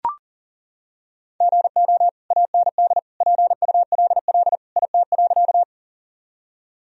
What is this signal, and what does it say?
Morse code, heard as a keyed tone, about 40 wpm, 700 Hz. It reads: GO AND PULL IT.
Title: Sentences from Top 1000 Words - Review Part 2 - 40wpm